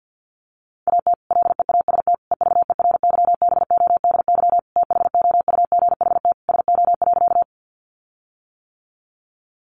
Morse code, heard as a keyed tone, about 50 wpm, 700 Hz. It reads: AT LEAST EVERYBODY THOUGHT SO.
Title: Sentences from Top 1400 Words 50wpm